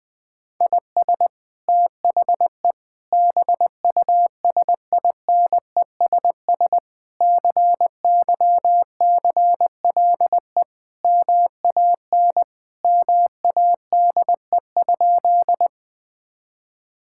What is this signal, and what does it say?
Morse code, heard as a keyed tone, about 21 wpm, 700 Hz. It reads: IS THE BUSINESS CYCLE MAN MADE?
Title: Sentences from Top 1000 Words - Review Part 6 - 20wpm